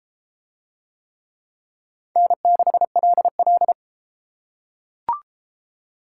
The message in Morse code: N6LL

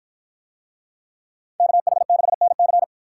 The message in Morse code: KH6NC

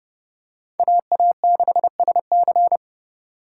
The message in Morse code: AA6SC

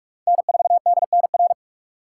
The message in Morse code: N4DNR

